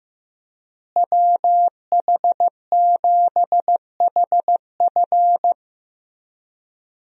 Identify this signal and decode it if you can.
Morse code, heard as a keyed tone, about 15 wpm, 700 Hz. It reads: WH7HF
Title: USA Call Signs - 5 Characters - Repeated 3x - 15wpm